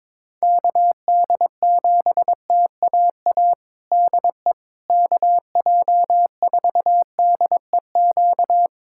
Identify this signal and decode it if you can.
Morse code, heard as a keyed tone, about 22 wpm, 700 Hz. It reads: KD7TAA DE KJ4DEQ
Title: USA Callsign DE Callsign 22wpm